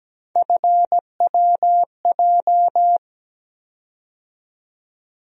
Morse code, heard as a keyed tone, about 17 wpm, 700 Hz. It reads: FWJ
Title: USA Call Sign Postfixes - 3 Letters 17wpm